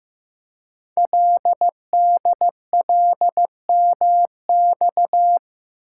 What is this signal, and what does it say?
Morse code, heard as a keyed tone, about 15 wpm, 700 Hz. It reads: LDLMX
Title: Code Groups - 5 Letters - 15wpm